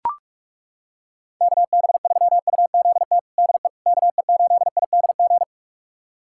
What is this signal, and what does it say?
Morse code, heard as a keyed tone, about 43 wpm, 700 Hz. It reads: KB3UZT DE KE8IDG
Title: USA Callsign DE Callsign 45wpm